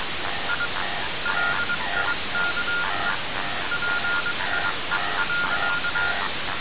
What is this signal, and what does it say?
Morse code, heard as a keyed tone, about 22 wpm, spaced as for 20 wpm, 1420 Hz, with noise. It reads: I LIKE PIE.
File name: radiocap / REC z